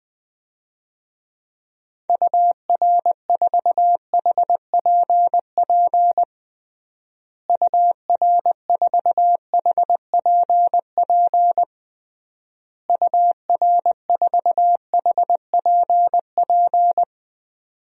Morse code, heard as a keyed tone, about 20 wpm, 700 Hz. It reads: UR4HPP UR4HPP UR4HPP
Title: DXCC-100 Call Signs - 6 Characters - Repeated 3x - 20wpm